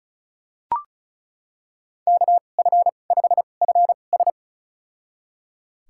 Morse code, heard as a keyed tone, about 34 wpm, spaced as for 25 wpm, 700 Hz. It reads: KF5FS
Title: USA Call Signs - 5 Characters - 2x Character Spacing - 35wpm